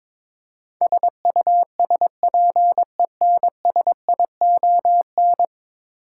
Morse code, heard as a keyed tone, about 21 wpm, 700 Hz. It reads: SUSPENSION
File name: practice / 22wpm / 10 Letter Words - First 3 Letters Given - 22wpm